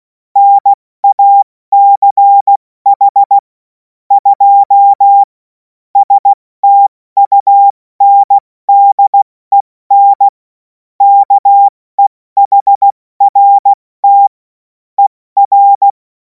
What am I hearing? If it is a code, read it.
Morse code, heard as a keyed tone, about 16 wpm, 800 Hz. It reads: NACH 2 STUNDEN KEHRT ER
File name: Morsekurs / Witze / Witz 07. April 2026 70 F80